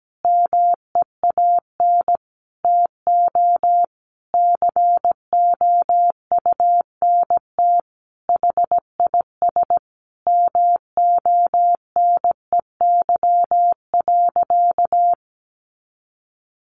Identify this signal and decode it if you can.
Morse code, heard as a keyed tone, about 17 wpm, 700 Hz. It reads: MEAN TO COUNT HIS MONEY.
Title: Sentences from Top 1500 Words 17wpm